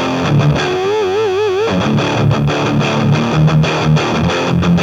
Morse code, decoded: none